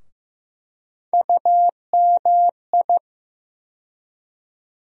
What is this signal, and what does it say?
Morse code, heard as a keyed tone, about 15 wpm, 700 Hz. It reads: UMI
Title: USA Call Sign Postfixes - 3 Letters 15wpm